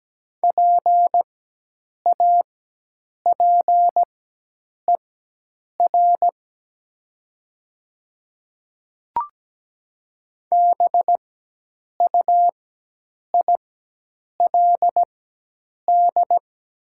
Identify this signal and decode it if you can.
Morse code, heard as a keyed tone, about 17 wpm, spaced as for 8 wpm, 700 Hz. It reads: PAPER BUILD
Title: 4x Character Spacing - Top 500 Words - 17wpm